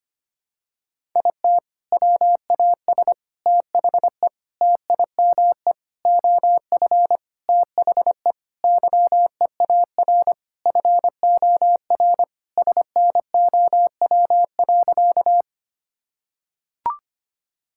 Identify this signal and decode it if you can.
Morse code, heard as a keyed tone, about 25 wpm, 700 Hz. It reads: IT WAS THE TIME OF THE YEAR FOR SNOW.
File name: practice / 25wpm / Sentences with 4-Word Phrases - 25wpm